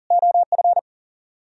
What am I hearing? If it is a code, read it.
Morse code, heard as a keyed tone, about 53 wpm, spaced as for 47 wpm, 700 Hz. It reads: OF
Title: Speed-Racing - Sets of 4 Words 40wpm